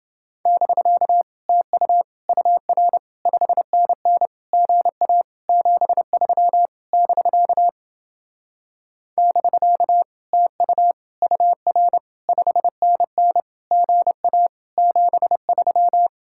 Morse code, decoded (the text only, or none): <BK> TU UR 5NN GA 73 <BK> <BK> TU UR 5NN GA 73